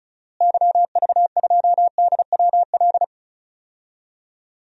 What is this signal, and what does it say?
Morse code, heard as a keyed tone, about 36 wpm, 700 Hz. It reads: YV2DWL
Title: DXCC-100 Call Signs - 6 Characters - Repeated 3x - 35wpm